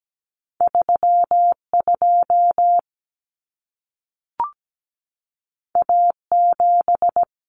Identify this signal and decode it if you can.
Morse code, heard as a keyed tone, about 17 wpm, 700 Hz. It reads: 32 A7
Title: Common Cut Numbers - 2-Digit 17wpm